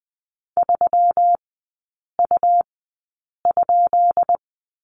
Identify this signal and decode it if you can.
Morse code, heard as a keyed tone, about 20 wpm, spaced as for 8 wpm, 700 Hz. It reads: 3U?
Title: 133 Lesson 22e - Question Mark - Three Characters 20wpm